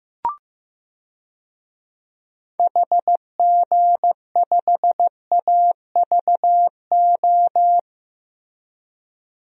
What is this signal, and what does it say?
Morse code, heard as a keyed tone, about 15 wpm, 700 Hz. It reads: HG5AVO
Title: DXCC-100 Call Signs - 6 Characters - Repeated 2x - 15wpm